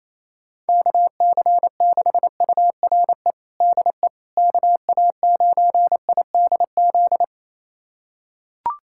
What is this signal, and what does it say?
Morse code, heard as a keyed tone, about 28 wpm, 700 Hz. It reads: KC6URE DE KA9IDZ
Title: USA Callsign DE Callsign - 6 Characters - 28wpm